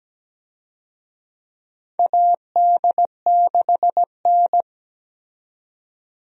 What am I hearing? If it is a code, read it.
Morse code, heard as a keyed tone, about 17 wpm, 700 Hz. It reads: AD6N
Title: USA Call Signs - 4 Characters - 17wpm